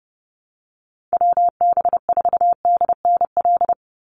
Morse code, heard as a keyed tone, about 30 wpm, 700 Hz. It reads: WB4DNL
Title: Call Signs - 30wpm